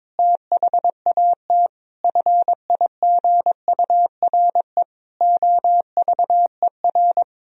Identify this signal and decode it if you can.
Morse code, heard as a keyed tone, about 22 wpm, 700 Hz. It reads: THAT FIGURE OVER
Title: Sentences from Top 1000 Words - Review Part 2 - 22wpm